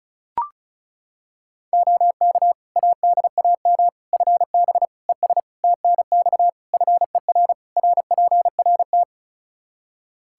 Morse code, heard as a keyed tone, about 36 wpm, 700 Hz. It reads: OK ADAM FB ES TNX FER RPRT
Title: Practice QSO - Exchange 2 - Info - 35wpm